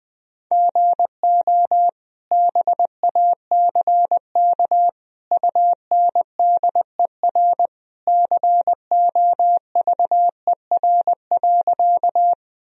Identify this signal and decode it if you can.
Morse code, heard as a keyed tone, about 20 wpm, 700 Hz. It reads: GO BACK UNDER COVER.
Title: Sentences from Top 1000 Words - Review Part 1 - 20wpm